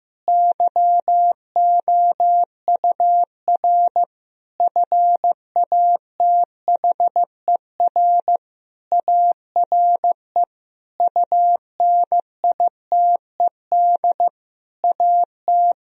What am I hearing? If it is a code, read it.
Morse code, heard as a keyed tone, about 15 wpm, 700 Hz. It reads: YOUR FATHER ARE UNITED AT
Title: Sentences from Top 1000 Words - Review Part 4 - 15wpm